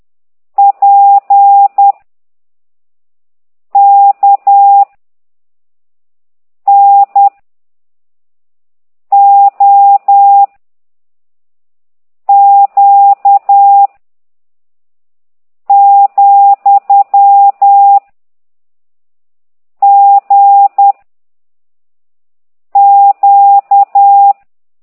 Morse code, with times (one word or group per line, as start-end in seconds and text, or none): 0.6-1.9 P
3.8-4.8 K
6.7-7.3 N
9.1-10.4 O
12.3-13.9 Q
15.7-18.0 ,
19.8-20.9 G
22.8-24.3 Q